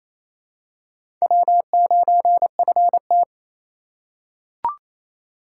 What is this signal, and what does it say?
Morse code, heard as a keyed tone, about 28 wpm, 700 Hz. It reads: W9FT
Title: USA Call Signs - 4 Characters - 3x Character Spacing - 28wpm